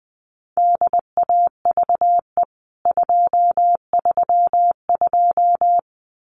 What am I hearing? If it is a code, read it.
Morse code, heard as a keyed tone, about 20 wpm, 700 Hz. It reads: DAVE 232
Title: CWT Exchange 20wpm